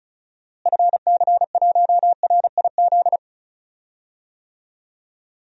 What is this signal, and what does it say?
Morse code, heard as a keyed tone, about 36 wpm, 700 Hz. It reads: FC1RIZ
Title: DXCC-100 Call Signs - 6 Characters - Repeated 2x - 35wpm